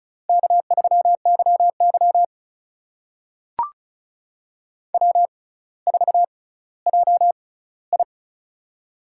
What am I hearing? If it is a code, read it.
Morse code, heard as a keyed tone, about 36 wpm, spaced as for 18 wpm, 700 Hz. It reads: K3YY W4JI
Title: USA Call Signs - 4 Characters - 6x Character Spacing - 35wpm